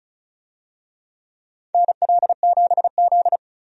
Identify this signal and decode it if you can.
Morse code, heard as a keyed tone, about 36 wpm, 700 Hz. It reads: NL7Z
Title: USA Call Signs - 4 Characters - 35wpm